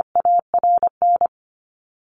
Morse code, heard as a keyed tone, about 25 wpm, 700 Hz. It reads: ARN
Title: Top 500 Words 25wpm